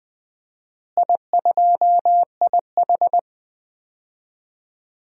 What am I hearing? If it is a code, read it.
Morse code, heard as a keyed tone, about 21 wpm, 700 Hz. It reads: I2IH
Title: DXCC-100 Call Signs - 4 Characters - Repeated 2x - 20wpm